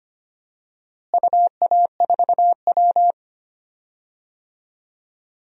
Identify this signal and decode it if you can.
Morse code, heard as a keyed tone, about 25 wpm, 700 Hz. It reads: UA4W
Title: DXCC-100 Call Signs - 4 Characters - Repeated 3x - 25wpm